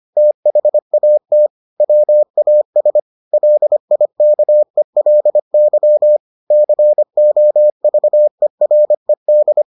Morse code, heard as a keyed tone, about 25 wpm, 600 Hz. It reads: THAT WAS LIKELY COVERED